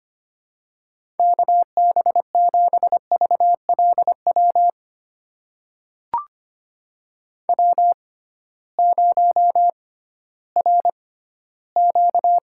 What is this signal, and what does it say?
Morse code, heard as a keyed tone, about 25 wpm, spaced as for 15 wpm, 700 Hz. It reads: KB7VLW W0RQ
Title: USA Call Signs - 6x Character Spacing - 25wpm